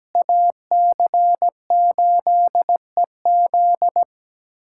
Morse code, heard as a keyed tone, about 17 wpm, 700 Hz. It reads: AC8EZ
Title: USA Callsign DE Callsign - 5 Characters - 17wpm